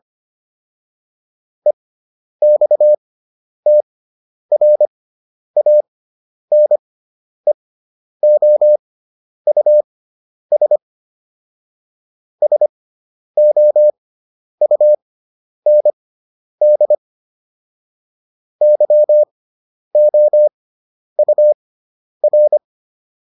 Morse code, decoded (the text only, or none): EXTRANEOUS SOUND YOUR